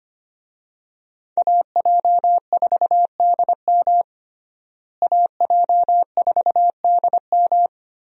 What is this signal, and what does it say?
Morse code, heard as a keyed tone, about 25 wpm, 700 Hz. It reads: AJ4DM AJ4DM
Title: USA Call Signs - 5 Characters - Repeated 2x - 25wpm